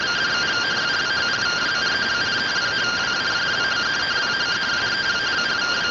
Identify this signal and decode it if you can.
Two-tone teleprinter signal, about 170 Hz shift.